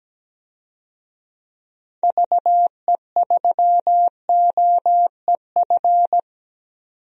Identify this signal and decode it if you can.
Morse code, heard as a keyed tone, about 17 wpm, 700 Hz. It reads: VE3OEF
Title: DXCC-100 Call Signs - 6 Characters - 17wpm